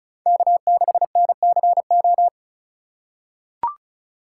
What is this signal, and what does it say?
Morse code, heard as a keyed tone, about 36 wpm, 700 Hz. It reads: K6NCO
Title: USA Call Signs - 5 Characters - 4x Character Spacing - 35wpm